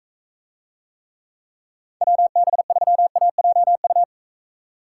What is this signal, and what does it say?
Morse code, heard as a keyed tone, about 42 wpm, 700 Hz. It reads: WB3AJU